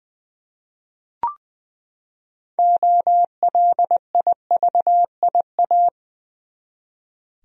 Morse code, heard as a keyed tone, about 20 wpm, 700 Hz. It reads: OLIVIA
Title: USA Common Names - 20wpm